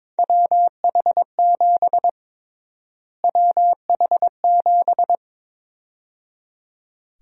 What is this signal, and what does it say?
Morse code, heard as a keyed tone, about 22 wpm, 700 Hz. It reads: WH7 WH7
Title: USA Call Sign Prefixes - Repeated 2x - 22wpm